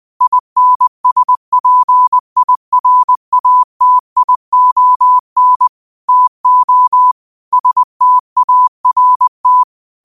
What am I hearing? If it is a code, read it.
Morse code, heard as a keyed tone, about 20 wpm, 1000 Hz. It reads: INSPIRATION TO START
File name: qotdfiles / 20wpm